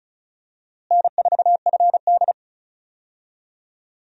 Morse code, heard as a keyed tone, about 36 wpm, 700 Hz. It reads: N4FD
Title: USA Call Signs - 4 Characters - Repeated 2x - 35wpm